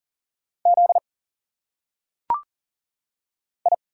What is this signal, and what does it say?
Morse code, heard as a keyed tone, about 40 wpm, 700 Hz. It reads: Z I
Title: Single Letters - Phonetic Pronunciation 40wpm